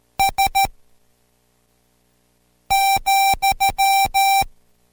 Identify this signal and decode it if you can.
Morse code, heard as a keyed tone, about 13 wpm, spaced as for 4 wpm, 780 Hz. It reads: S,